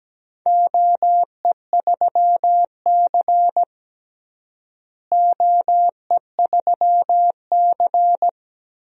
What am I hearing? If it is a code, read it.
Morse code, heard as a keyed tone, about 17 wpm, 700 Hz. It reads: OE3C OE3C
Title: DXCC-100 Call Signs - 4 Characters - Repeated 2x - 17wpm